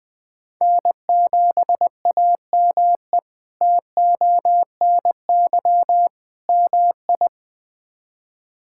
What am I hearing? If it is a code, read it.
Morse code, heard as a keyed tone, about 20 wpm, 700 Hz. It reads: N7AME TONY MI
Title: K1USN SST QSO Practice - 20wpm